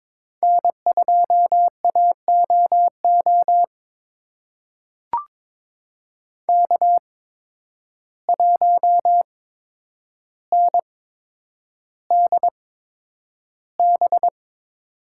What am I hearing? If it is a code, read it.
Morse code, heard as a keyed tone, about 23 wpm, spaced as for 9 wpm, 700 Hz. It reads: N2AOO K1NDB
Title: USA Call Signs - 5 Characters - 8x Character Spacing - 22wpm